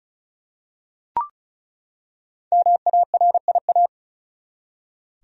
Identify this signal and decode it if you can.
Morse code, heard as a keyed tone, about 36 wpm, 700 Hz. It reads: MARIA
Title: USA Common Names - 35wpm